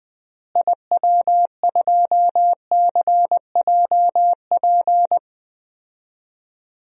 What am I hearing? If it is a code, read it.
Morse code, heard as a keyed tone, about 20 wpm, 700 Hz. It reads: IW2CJP
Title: DXCC-100 Call Signs - 6 Characters - Repeated 3x - 20wpm